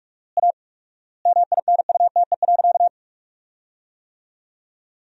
Morse code, heard as a keyed tone, about 46 wpm, 700 Hz. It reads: A MINUTE.
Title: 4x Word-Spacing - Sentences from Top 500 Words - 45wpm